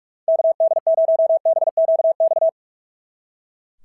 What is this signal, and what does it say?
Morse code, heard as a keyed tone, about 41 wpm, 690 Hz. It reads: KD0BQX
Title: SOTA QSO Practice 45wpm